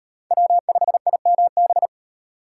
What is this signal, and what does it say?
Morse code, heard as a keyed tone, about 35 wpm, 700 Hz. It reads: W5IMB